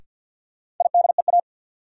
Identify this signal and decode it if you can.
Morse code, heard as a keyed tone, about 51 wpm, 700 Hz. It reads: IDEA
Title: Sets of 3 Words - New Word Drill - 50wpm